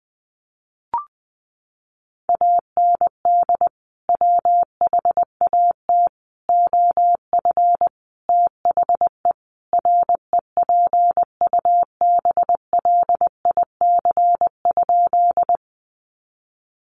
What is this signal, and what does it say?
Morse code, heard as a keyed tone, about 20 wpm, 700 Hz. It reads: AND WHAT OF THE REPUBLIC?
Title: Sentences from Top 1500 Words 20wpm